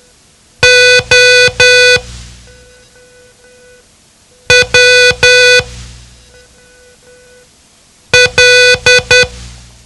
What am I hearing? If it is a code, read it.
Morse code, heard as a keyed tone, about 10 wpm, spaced as for 3 wpm, 500 Hz. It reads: OWL
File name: ham radio / CodeQuick / Code Quick 07